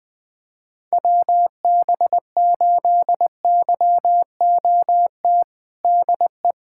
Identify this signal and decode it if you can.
Morse code, heard as a keyed tone, about 20 wpm, 700 Hz. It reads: WB8YOT DE
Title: USA Callsign DE Callsign 20wpm